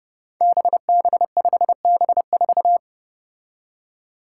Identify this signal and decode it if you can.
Morse code, heard as a keyed tone, about 29 wpm, 700 Hz. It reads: BB5B4
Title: Five Challenging Letter-Number Combinations 30wpm